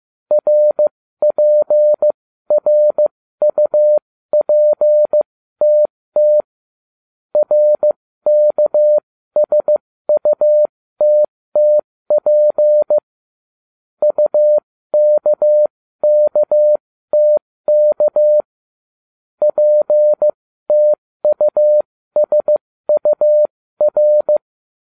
Morse code, heard as a keyed tone, about 15 wpm, spaced as for 13 wpm, 600 Hz. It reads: RPRUPTT RKSUTTP UKKTK PTUSUR
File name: cw / ubacw07 1